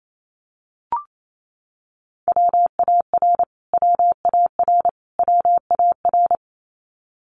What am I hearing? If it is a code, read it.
Morse code, heard as a keyed tone, about 28 wpm, 700 Hz. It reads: WAR WAR WAR
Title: Top 500 Words Repeated 28wpm